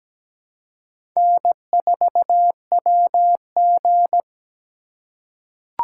Morse code, heard as a keyed tone, about 17 wpm, 700 Hz. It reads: N4WG
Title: USA Call Signs - 4 Characters - 2x Character Spacing - 17wpm